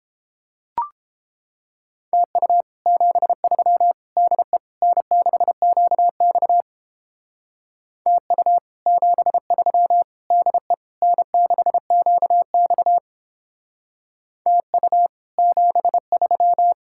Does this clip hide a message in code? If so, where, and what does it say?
Yes: Morse code from 2.1 to 16.7 s, heard as a keyed tone, about 29 wpm, 700 Hz. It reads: TU 73 DE N6QX TU 73 DE N6QX TU 73